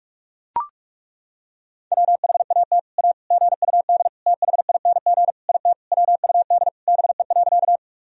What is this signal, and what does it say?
Morse code, heard as a keyed tone, about 41 wpm, 700 Hz. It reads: WHAT A GUD THING IT WUD BE.
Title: Sentences from Top 200 Words and Abbreviations 45wpm